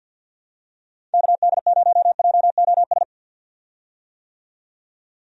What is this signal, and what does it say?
Morse code, heard as a keyed tone, about 51 wpm, 700 Hz. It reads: KD0JOS
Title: USA Call Signs - 6 Characters - 50wpm